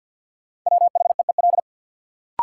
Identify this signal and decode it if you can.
Morse code, heard as a keyed tone, about 50 wpm, 700 Hz. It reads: WHEEL
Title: Sentences from Top 500 Words - New Word Drill - 50wpm